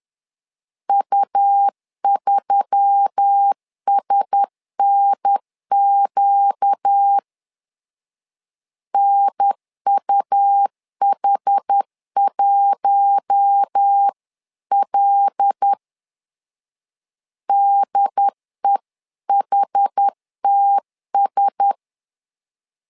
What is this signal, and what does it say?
Morse code, heard as a keyed tone, about 10 wpm, 800 Hz. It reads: U3SNQ NUH1L DEHTS
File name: LEZIONI / 5 LBVC / 5 L 10wpm 2259